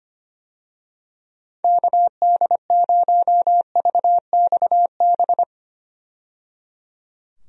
Morse code, heard as a keyed tone, about 25 wpm, 700 Hz. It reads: KD0VXB
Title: USA Call Signs - 6 Characters - 25wpm